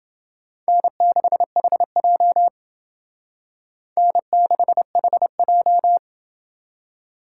Morse code, heard as a keyed tone, about 28 wpm, 700 Hz. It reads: N6HJ N6HJ